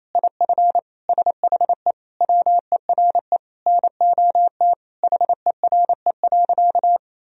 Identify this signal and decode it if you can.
Morse code, heard as a keyed tone, about 28 wpm, 700 Hz. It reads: IF SHE WERE NOT HERE.
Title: Sentences from Top 1000 Words - Review Part 1 - 28wpm